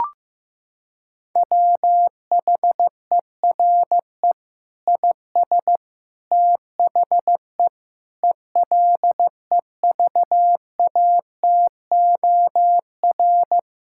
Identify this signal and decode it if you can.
Morse code, heard as a keyed tone, about 15 wpm, 700 Hz. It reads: WHERE IS THE ELEVATOR
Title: Sentences from Top 1600 Words 15wpm